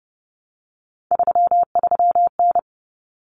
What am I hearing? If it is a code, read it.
Morse code, heard as a keyed tone, about 30 wpm, 700 Hz. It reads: 33N
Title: Common Cut Numbers - 3-Digit 30wpm